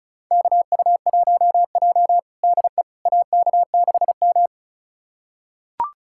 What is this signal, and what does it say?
Morse code, heard as a keyed tone, about 36 wpm, 700 Hz. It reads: KU1J DE AK6M
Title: USA Callsign DE Callsign - 4 Characters - 35wpm